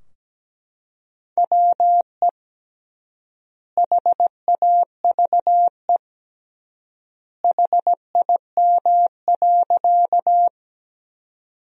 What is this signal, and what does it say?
Morse code, heard as a keyed tone, about 17 wpm, 700 Hz. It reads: WE HAVE HIM.
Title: 3x Word-Spacing - Sentences from Top 100 Words - 17wpm